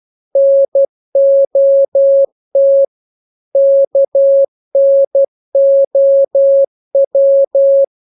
Morse code, heard as a keyed tone, about 12 wpm, 550 Hz. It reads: NOT KNOW
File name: jokes / joke 14 12 0000